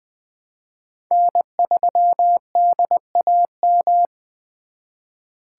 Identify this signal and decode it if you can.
Morse code, heard as a keyed tone, about 24 wpm, 700 Hz. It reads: N3DAM
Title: Speed Racing - USA 1x3 Call Signs 20wpm